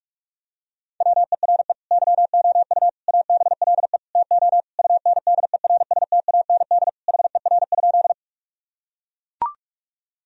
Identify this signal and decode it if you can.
Morse code, heard as a keyed tone, about 41 wpm, 700 Hz. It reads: WERE YOU ABLE TO UNDERSTAND HER?
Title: Sentences from Top 1000 Words - Review Part 2 - 45wpm